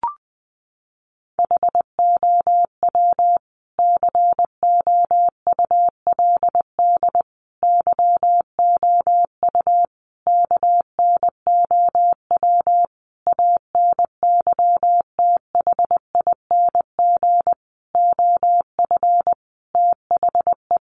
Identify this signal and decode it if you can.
Morse code, heard as a keyed tone, about 20 wpm, 700 Hz. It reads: HOW COULD YOU KNOW ANYTHING OF THE
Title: Sentences from Top 1000 Words 20wpm